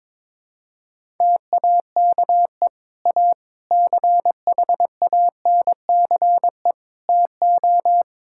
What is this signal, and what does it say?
Morse code, heard as a keyed tone, about 21 wpm, 700 Hz. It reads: TAKE A CHANCE TO
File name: practice / 22wpm / Sentences with 3-Word Phrases - 22wpm